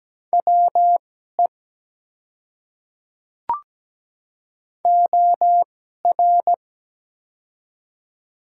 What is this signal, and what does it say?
Morse code, heard as a keyed tone, about 17 wpm, spaced as for 12 wpm, 700 Hz. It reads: WE OR
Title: 2x Character Spacing - Top 300 Words - 17wpm